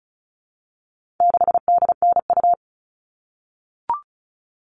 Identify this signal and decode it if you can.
Morse code, heard as a keyed tone, about 34 wpm, 700 Hz. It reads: 6DNU
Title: All Cut Numbers - 4-Digit 35wpm